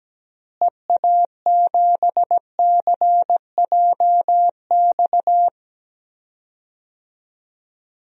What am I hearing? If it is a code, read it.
Morse code, heard as a keyed tone, about 17 wpm, 700 Hz. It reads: EA7CJX
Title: DXCC-100 Call Signs - 6 Characters - Repeated 3x - 17wpm